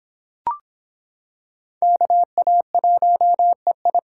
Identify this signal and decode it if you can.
Morse code, heard as a keyed tone, about 26 wpm, 700 Hz. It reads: KA1EI